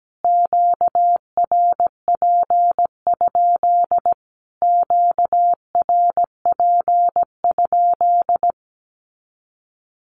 Morse code, heard as a keyed tone, about 17 wpm, 700 Hz. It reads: QRP? QRP?
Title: Q-Codes Repeated 3x 17wpm